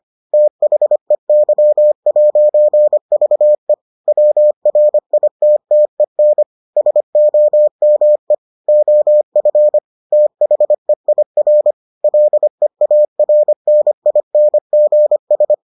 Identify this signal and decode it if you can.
Morse code, heard as a keyed tone, about 25 wpm, 600 Hz. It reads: THEY'VE WRITTEN SOME OF THEIR LEARNINGS